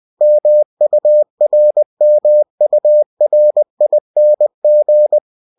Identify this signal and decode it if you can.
Morse code, heard as a keyed tone, about 20 wpm, 600 Hz. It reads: MURMURING
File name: cw-books / winnie-the-pooh / winnie-20wpm-page-0039